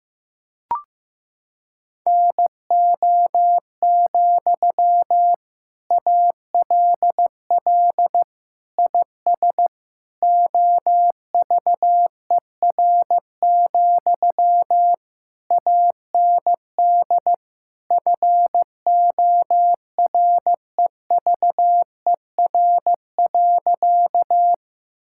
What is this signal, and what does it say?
Morse code, heard as a keyed tone, about 15 wpm, 700 Hz. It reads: NO, ALL IS OVER, AND FOREVER.
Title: Sentences from Top 1600 Words 15wpm